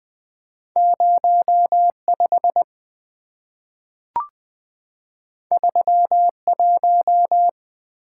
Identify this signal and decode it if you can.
Morse code, heard as a keyed tone, about 20 wpm, 690 Hz. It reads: 05 31
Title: Two Letter-Number Combinations - 20wpm